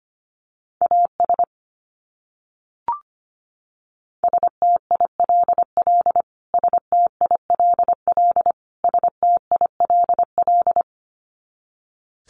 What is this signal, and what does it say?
Morse code, heard as a keyed tone, about 25 wpm, 700 Hz. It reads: AS STILL STILL STILL
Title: Top 300 Words Repeated 25wpm